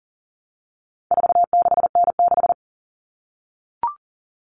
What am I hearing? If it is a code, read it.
Morse code, heard as a keyed tone, about 40 wpm, 700 Hz. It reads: 46N6
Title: Common Cut Numbers - 4-Digit 40wpm